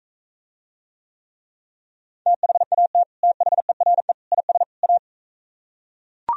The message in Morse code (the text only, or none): THAT THERE IS A